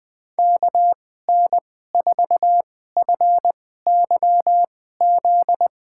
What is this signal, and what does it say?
Morse code, heard as a keyed tone, about 20 wpm, spaced as for 14 wpm, 700 Hz. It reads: KN4FYZ
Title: USA Call Signs - 6 Characters - 2x Character Spacing - 20wpm